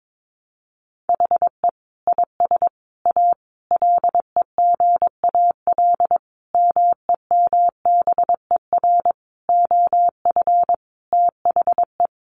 Morse code, heard as a keyed tone, about 22 wpm, 700 Hz. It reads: HE IS A LEGAL MEMBER OF THE
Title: Sentences from Top 1200 Words 22wpm